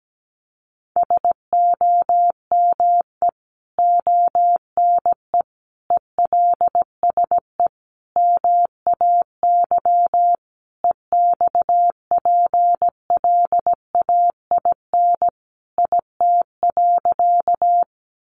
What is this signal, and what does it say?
Morse code, heard as a keyed tone, about 17 wpm, 700 Hz. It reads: SOME ONE ELSE MAY EXPLAIN IT.
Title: Sentences from Top 1100 Words 17wpm